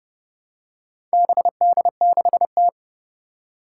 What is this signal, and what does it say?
Morse code, heard as a keyed tone, about 30 wpm, 700 Hz. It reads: BD6T
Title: Four Challenging Letter-Number Combinations 30wpm